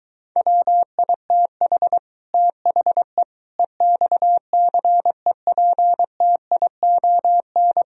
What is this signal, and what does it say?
Morse code, heard as a keyed tone, about 24 wpm, 700 Hz. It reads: WITH THE EXCEPTION